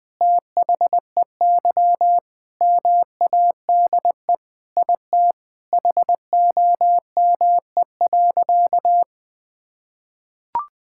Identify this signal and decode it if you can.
Morse code, heard as a keyed tone, about 20 wpm, 700 Hz. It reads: THEY MADE IT HOME.
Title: Sentences with 4-Word Phrases - 20wpm